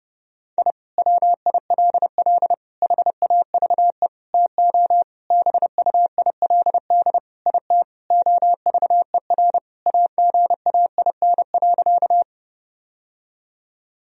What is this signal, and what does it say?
Morse code, heard as a keyed tone, about 30 wpm, 700 Hz. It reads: I WILL HAVE TO BUILD IT OVER AGAIN.
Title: Sentences from Top 1000 Words - Review Part 1 - 30wpm